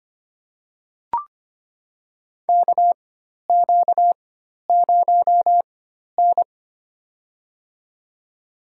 Morse code, heard as a keyed tone, about 25 wpm, spaced as for 12 wpm, 700 Hz. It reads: KQ0N
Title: USA Call Signs - 4 Characters - 4x Character Spacing - 25wpm